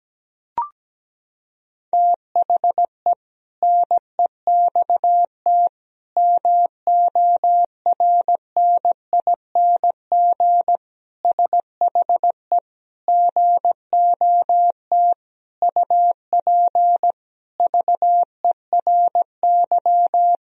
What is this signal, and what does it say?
Morse code, heard as a keyed tone, about 17 wpm, 700 Hz. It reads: THE NEXT MORNING SHE GOT UP VERY
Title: Sentences from Top 1000 Words - Review Part 2 - 17wpm